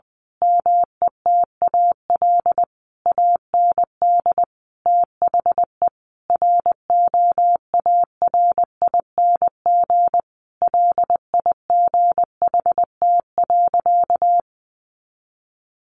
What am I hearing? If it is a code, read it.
Morse code, heard as a keyed tone, about 20 wpm, 700 Hz. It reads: METAL AND THE ROARING LIGHT.
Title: A Sound Of Thunder 20wpm